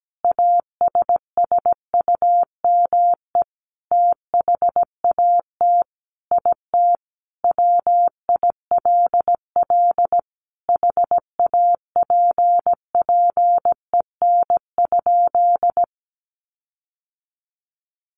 Morse code, heard as a keyed tone, about 17 wpm, 700 Hz. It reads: ASSUME THAT IT WILL HAPPEN?
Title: Sentences from Top 1500 Words 17wpm